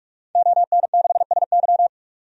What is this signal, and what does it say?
Morse code, heard as a keyed tone, about 46 wpm, 700 Hz. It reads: ON6SY
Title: DXCC-100 Call Signs - 5 Characters - 45wpm